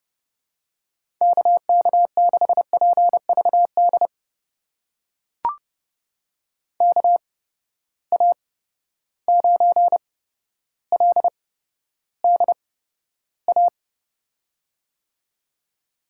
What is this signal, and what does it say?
Morse code, heard as a keyed tone, about 32 wpm, spaced as for 13 wpm, 700 Hz. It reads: KK6PVD KA9LDA
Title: USA Call Signs - 6 Characters - 8x Character Spacing - 30wpm